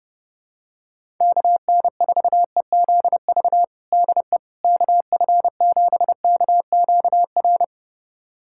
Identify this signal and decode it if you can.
Morse code, heard as a keyed tone, about 32 wpm, 700 Hz. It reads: KN4EZV DE KF7KQR
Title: USA Callsign DE Callsign 30wpm